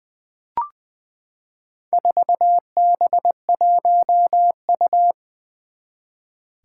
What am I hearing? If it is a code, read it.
Morse code, heard as a keyed tone, about 20 wpm, 700 Hz. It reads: 4B1U
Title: Four Challenging Letter-Number Combinations 20wpm